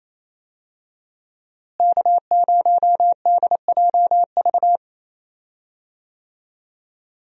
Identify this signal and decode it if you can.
Morse code, heard as a keyed tone, about 28 wpm, 700 Hz. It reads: K0DJV
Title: USA Call Signs - 5 Characters - Repeated 2x - 28wpm